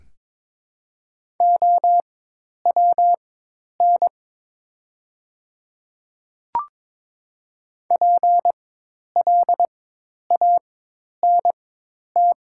Morse code, heard as a keyed tone, about 22 wpm, spaced as for 10 wpm, 700 Hz. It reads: OWN PLANT
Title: 4x Character Spacing - Top 500 Words - 22wpm